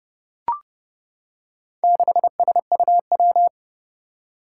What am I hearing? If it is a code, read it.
Morse code, heard as a keyed tone, about 30 wpm, 700 Hz. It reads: 6SUW
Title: Four Challenging Letter-Number Combinations 30wpm